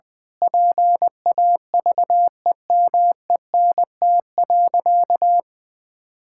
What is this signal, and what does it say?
Morse code, heard as a keyed tone, about 20 wpm, 700 Hz. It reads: PAVEMENT.
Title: Pangram Challenge v2 - 20wpm